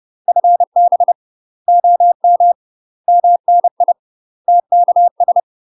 Morse code, heard as a keyed tone, about 30 wpm, 700 Hz. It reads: FB OM MNI TKS
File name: UK MASTER QSOs TRAINING / 30WPM / UK MASTER QSOs 0041